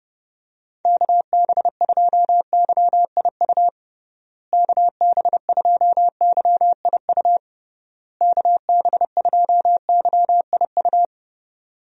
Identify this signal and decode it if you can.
Morse code, heard as a keyed tone, about 32 wpm, 700 Hz. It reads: KB2YIU KB2YIU KB2YIU
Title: USA Call Signs - 6 Characters - Repeated 3x - 30wpm